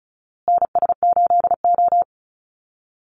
Morse code, heard as a keyed tone, about 34 wpm, 700 Hz. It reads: NS8O
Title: Call Signs Repeated 2x - 35wpm